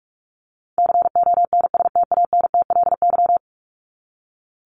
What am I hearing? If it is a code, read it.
Morse code, heard as a keyed tone, about 44 wpm, 700 Hz. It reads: CONSTANTLY
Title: LY Suffix - Top 100 Words with Base Words 45wpm